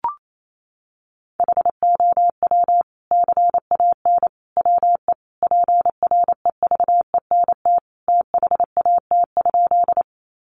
Morse code, heard as a keyed tone, about 28 wpm, 700 Hz. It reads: HOW CAN WE PREVENT THAT?
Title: Sentences from Top 1100 Words 28wpm